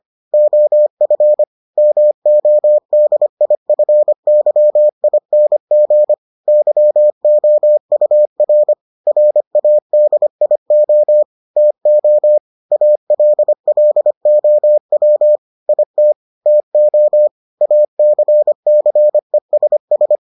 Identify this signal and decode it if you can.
Morse code, heard as a keyed tone, about 25 wpm, 600 Hz. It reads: OF MODIFYING YOUR RADIO TO ALLOW IT TO ACCESS